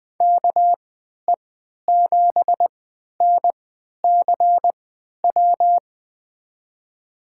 Morse code, heard as a keyed tone, about 20 wpm, spaced as for 11 wpm, 700 Hz. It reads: KE7NCW
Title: USA Call Signs - 6 Characters - 3x Character Spacing - 20wpm